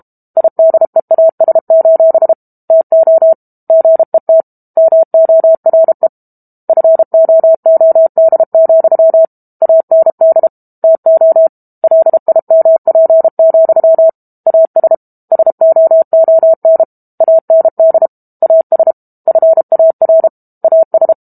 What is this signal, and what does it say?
Morse code, heard as a keyed tone, about 32 wpm, 650 Hz. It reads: IDEAS: TO GET MORE FOOD, AND TO LIMP, AS SOON AND AS FAR AS
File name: Chapter 6 - 32 WPM